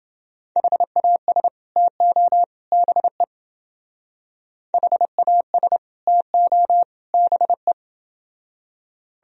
Morse code, heard as a keyed tone, about 29 wpm, 700 Hz. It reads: HAS TO BE HAS TO BE